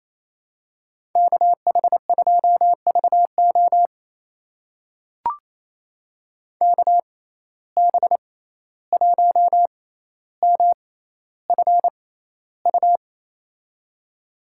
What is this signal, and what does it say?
Morse code, heard as a keyed tone, about 28 wpm, spaced as for 14 wpm, 700 Hz. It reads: KH2VO KB1MFU
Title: USA Call Signs - 6x Character Spacing - 28wpm